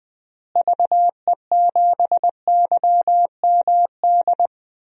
Morse code, heard as a keyed tone, about 20 wpm, 700 Hz. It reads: VE7YMD
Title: DXCC-100 Call Signs - 6 Characters - Repeated 2x - 20wpm